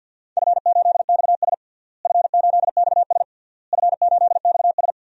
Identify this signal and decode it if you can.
Morse code, heard as a keyed tone, about 52 wpm, 700 Hz. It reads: F8XS F8XS F8XS
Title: DXCC-100 Call Signs - 4 Characters - Repeated 3x - 50wpm